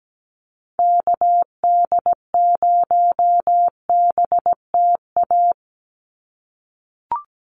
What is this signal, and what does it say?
Morse code, heard as a keyed tone, about 17 wpm, 700 Hz. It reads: KD0BTA
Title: USA Call Signs 17wpm